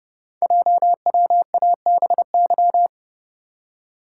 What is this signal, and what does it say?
Morse code, heard as a keyed tone, about 30 wpm, 700 Hz. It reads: JWABY
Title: Code Groups - 5 Letters - 30wpm